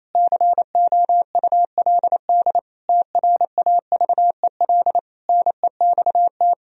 Morse code, heard as a keyed tone, about 28 wpm, 700 Hz. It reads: COULD TRAVEL NEXT
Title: Sentences from Top 1000 Words - Review Part 2 - 28wpm